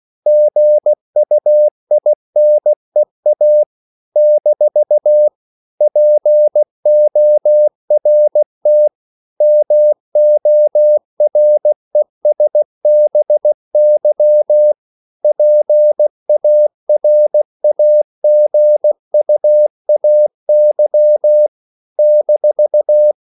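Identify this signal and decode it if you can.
Morse code, heard as a keyed tone, about 16 wpm, 600 Hz. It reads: GUINEA - PORT MORESBY PARAGUAY -
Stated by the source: CW practice of national capitals at 16 words per minute, from ON6ZQ.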